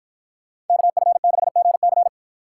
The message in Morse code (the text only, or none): KF6GC